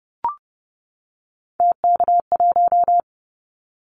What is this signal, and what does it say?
Morse code, heard as a keyed tone, about 30 wpm, 700 Hz. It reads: TK1
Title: Callsign Prefixes for Top 100 DXCC - 30wpm